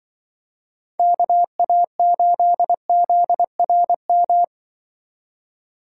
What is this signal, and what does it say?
Morse code, heard as a keyed tone, about 23 wpm, 700 Hz. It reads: KA8ZRM
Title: Speed Racing - USA 2x3 Call Signs 20wpm